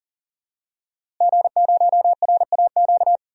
Morse code, heard as a keyed tone, about 40 wpm, 700 Hz. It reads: G0RAQ
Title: DXCC-100 Call Signs - 5 Characters - Repeated 2x - 40wpm